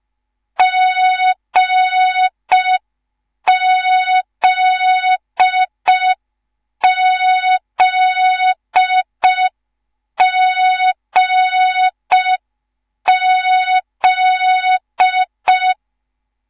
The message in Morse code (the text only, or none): GZZGZ